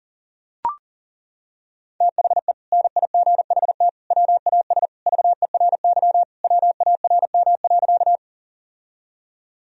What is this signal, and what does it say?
Morse code, heard as a keyed tone, about 40 wpm, 700 Hz. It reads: THE NIGHT WAS VERY WARM.
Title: Sentences from Top 1000 Words - Review Part 2 - 40wpm